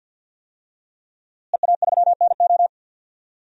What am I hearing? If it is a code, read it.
Morse code, heard as a keyed tone, about 46 wpm, 700 Hz. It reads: EA3NO
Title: DXCC-100 Call Signs - 5 Characters - Repeated 2x - 50wpm